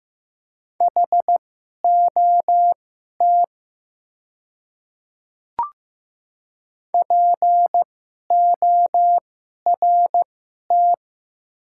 Morse code, heard as a keyed tone, about 15 wpm, spaced as for 11 wpm, 700 Hz. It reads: HOT PORT
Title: 2x Character Spacing - Top 200 Words - 15wpm